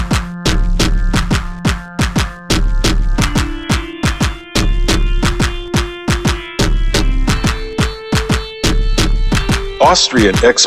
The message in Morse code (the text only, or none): none